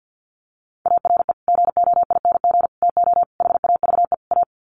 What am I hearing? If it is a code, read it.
Morse code, heard as a keyed tone, about 50 wpm, 700 Hz. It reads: ARE GOING TO HAVE A
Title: Sets of 5 Words Encore2 50wpm